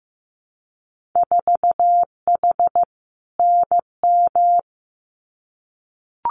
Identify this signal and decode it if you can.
Morse code, heard as a keyed tone, about 15 wpm, 700 Hz. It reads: 4H NM
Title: Winter Field Day Exchanges Encore 15wpm